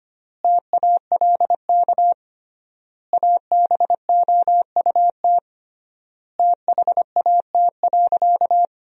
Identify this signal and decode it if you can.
Morse code, heard as a keyed tone, about 25 wpm, 700 Hz. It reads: TALK ABOUT THAT.
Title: 3x Word-Spacing - Sentences from Top 400 Words - 25wpm